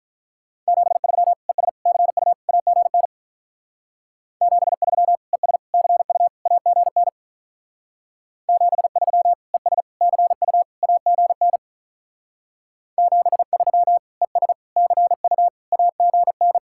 Morse code, heard as a keyed tone, about 44 wpm, 700 Hz. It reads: 73 ES CU AGN 73 ES CU AGN 73 ES CU AGN 73 ES CU AGN
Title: Speed-Racing Common QSO Phrases - 35wpm